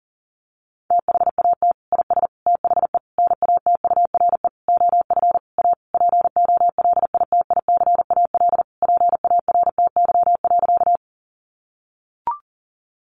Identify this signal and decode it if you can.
Morse code, heard as a keyed tone, about 40 wpm, 700 Hz. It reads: THAT IS THE NATURE OF A POLITICAL PARTY.
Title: Sentences from Top 1400 Words 40wpm